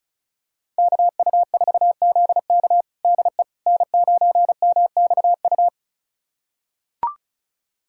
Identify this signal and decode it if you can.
Morse code, heard as a keyed tone, about 36 wpm, 700 Hz. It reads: KU4ZK DE N9MXU
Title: USA Callsign DE Callsign - 5 Characters - 35wpm